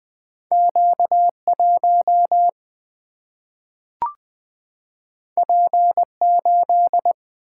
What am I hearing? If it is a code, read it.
Morse code, heard as a keyed tone, about 20 wpm, 700 Hz. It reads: Q 1 P 8
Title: Single-Letter Single-Number 20wpm